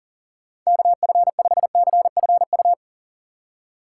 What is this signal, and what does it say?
Morse code, heard as a keyed tone, about 40 wpm, 700 Hz. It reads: KF5CFU
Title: USA Call Signs - 6 Characters - 3x Character Spacing - 40wpm